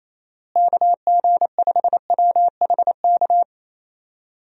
Morse code, heard as a keyed tone, about 27 wpm, 700 Hz. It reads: KG5WHK
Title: USA Call Signs - 6 Characters - Repeated 2x - 28wpm